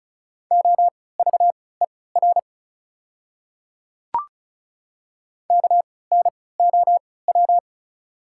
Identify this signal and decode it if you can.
Morse code, heard as a keyed tone, about 36 wpm, spaced as for 20 wpm, 690 Hz. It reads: OVER KNOW
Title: 3x Character Spacing - Top 100 Words - 35wpm